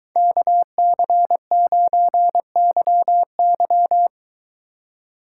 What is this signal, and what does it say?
Morse code, heard as a keyed tone, about 23 wpm, 700 Hz. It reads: KC9YY